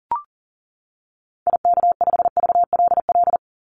Morse code, heard as a keyed tone, about 40 wpm, 700 Hz. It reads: IK5VLL
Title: Call Signs - 40wpm